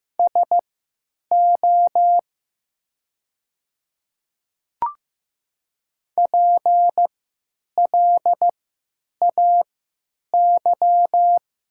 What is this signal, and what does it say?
Morse code, heard as a keyed tone, about 15 wpm, spaced as for 9 wpm, 700 Hz. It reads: SO PLAY
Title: 3x Character Spacing - Top 200 Words - 15wpm